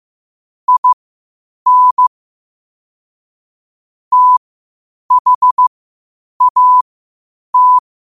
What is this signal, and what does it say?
Morse code, heard as a keyed tone, about 15 wpm, spaced as for 8 wpm, 1000 Hz. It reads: IN THAT
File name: qotdfiles / 8wpm